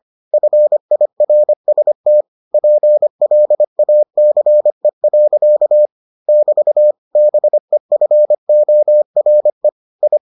Morse code, heard as a keyed tone, about 25 wpm, 600 Hz. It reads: FIRST PLACE. <BT> BEFORE I